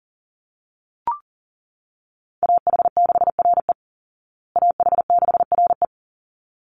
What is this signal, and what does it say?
Morse code, heard as a keyed tone, about 40 wpm, 700 Hz. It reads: AH6RE AH6RE
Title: Call Signs Repeated 3x - 40wpm